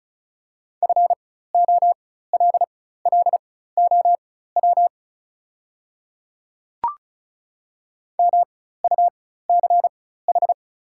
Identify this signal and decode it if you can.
Morse code, heard as a keyed tone, about 34 wpm, spaced as for 16 wpm, 700 Hz. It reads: FOLLOW MUCH
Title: 4x Character Spacing - Top 200 Words - 35wpm